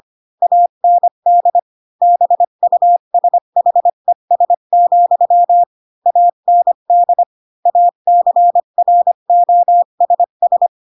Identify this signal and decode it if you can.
Morse code, heard as a keyed tone, about 25 wpm, 700 Hz. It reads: AND BUSHES, AND ACROSS
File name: Pinocchio / Pinocchio ch4 20 25 ik1hge